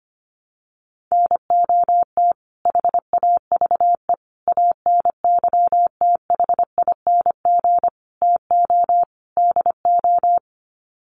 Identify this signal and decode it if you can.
Morse code, heard as a keyed tone, about 25 wpm, 700 Hz. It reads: NOT HAVE ANYTHING TO DO
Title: Sets of 5 Words Encore 25wpm